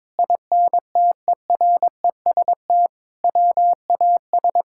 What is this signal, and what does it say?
Morse code, heard as a keyed tone, about 22 wpm, 700 Hz. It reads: INTEREST WAS
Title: Sentences from Top 1000 Words - Review Part 6 - 22wpm